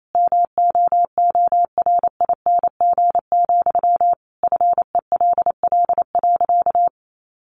Morse code, heard as a keyed tone, about 28 wpm, 700 Hz. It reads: MOORING, FELL.
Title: A Sound Of Thunder 28wpm